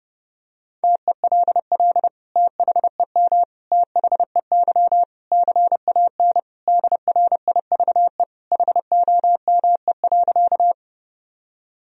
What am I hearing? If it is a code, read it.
Morse code, heard as a keyed tone, about 30 wpm, 700 Hz. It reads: TELL THEM THEY CAN DRIVE HOME.
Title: Sentences from Top 1000 Words - Review Part 2 - 30wpm